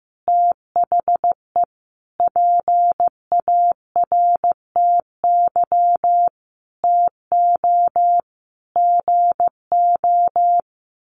Morse code, heard as a keyed tone, about 15 wpm, 700 Hz. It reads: THE PARTY TO GO
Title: Sentences from Top 1400 Words 15wpm